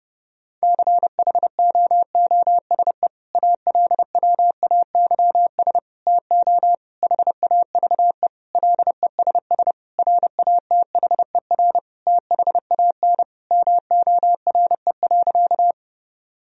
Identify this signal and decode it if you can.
Morse code, heard as a keyed tone, about 30 wpm, 700 Hz. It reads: CHOOSE ALWAYS TO HAVE LESS RATHER THAN MORE.
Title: Sentences from Top 1000 Words - Review Part 4 - 30wpm